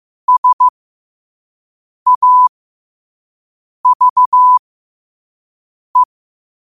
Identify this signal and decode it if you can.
Morse code, heard as a keyed tone, about 15 wpm, spaced as for 5 wpm, 1000 Hz. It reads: SAVE